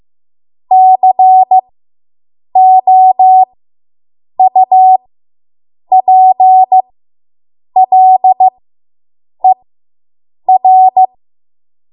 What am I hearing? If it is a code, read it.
Morse code, heard as a keyed tone, about 15 wpm, spaced as for 7 wpm, 750 Hz. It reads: COUPLER